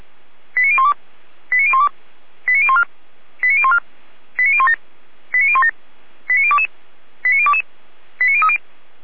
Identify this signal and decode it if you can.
Five-tone selective call (ZVEI), heard as quick runs of five tones, tones of about 70 ms. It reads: 89012 89012 89014 89014 89017 89017 89022 89022 89030